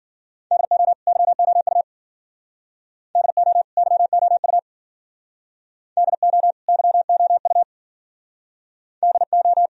Morse code, heard as a keyed tone, about 50 wpm, 700 Hz. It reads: DO YOU DO YOU DO YOU DO
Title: Speed-Racing - Sets of 2 Words 40wpm